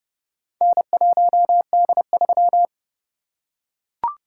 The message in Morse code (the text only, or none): N1D3